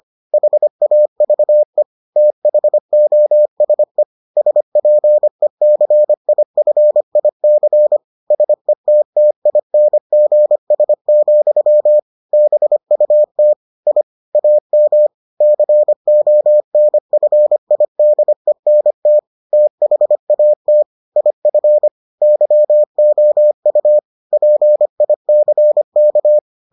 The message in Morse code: HAVE THOSE SPECIFIC SETTINGS, BUT I AM CONFIDENT THAT IF YOU PICK

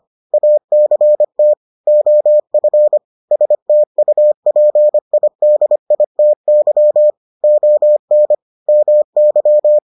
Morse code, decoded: ACT OF STUPIDITY ON MY

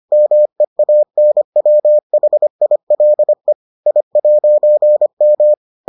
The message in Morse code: MEANWHILE I'M

